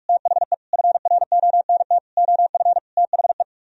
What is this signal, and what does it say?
Morse code, heard as a keyed tone, about 45 wpm, 700 Hz. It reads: THE FRONT OF THE